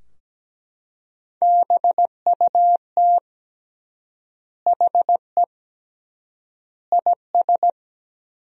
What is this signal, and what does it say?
Morse code, heard as a keyed tone, about 17 wpm, 700 Hz. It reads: BUT HE IS
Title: 3x Word-Spacing - Sentences from Top 200 Words - 17wpm